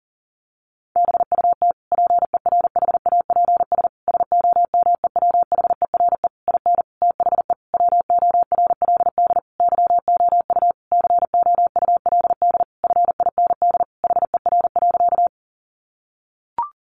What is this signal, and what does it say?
Morse code, heard as a keyed tone, about 40 wpm, 700 Hz. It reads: BUT PERHAPS SOMEWHERE IN THE WORLD YOU COULD FIND HER.
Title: Sentences from Top 1400 Words 40wpm